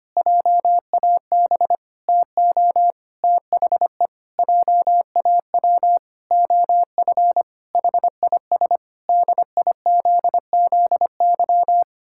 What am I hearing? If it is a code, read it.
Morse code, heard as a keyed tone, about 25 wpm, 700 Hz. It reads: JAB TO THE JAW OF HIS DIZZY
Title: Pangram Challenge v2 - 25wpm